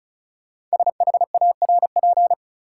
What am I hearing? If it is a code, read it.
Morse code, heard as a keyed tone, about 35 wpm, 700 Hz. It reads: SHARP
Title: Five Letter Words - 35wpm